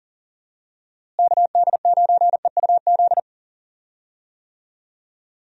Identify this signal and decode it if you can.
Morse code, heard as a keyed tone, about 40 wpm, 700 Hz. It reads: KD9EUZ
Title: USA Call Signs - 6 Characters - Repeated 3x - 40wpm